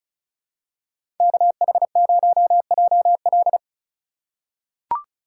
Morse code, heard as a keyed tone, about 34 wpm, 700 Hz. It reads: KH0JL
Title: USA Call Signs - 5 Characters - 3x Character Spacing - 35wpm